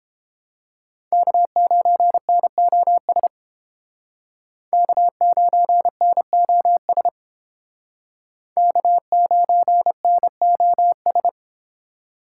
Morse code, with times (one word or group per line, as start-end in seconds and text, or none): 1.1-3.3 K9NOS
4.7-7.1 K9NOS
8.6-11.3 K9NOS